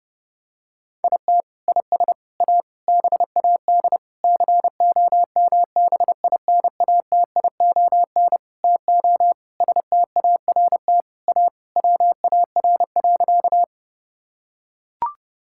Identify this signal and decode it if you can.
Morse code, heard as a keyed tone, about 30 wpm, 700 Hz. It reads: IT IS A BAD COMBINATION TO START A WAR.
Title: Sentences from Top 1000 Words - Review Part 6 - 30wpm